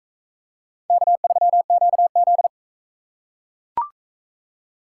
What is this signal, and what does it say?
Morse code, heard as a keyed tone, about 37 wpm, 700 Hz. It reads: K3QZ